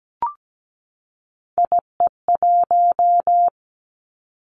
Morse code, heard as a keyed tone, about 17 wpm, 700 Hz. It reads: IE1
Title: Three Challenging Letter-Number Combinations 17wpm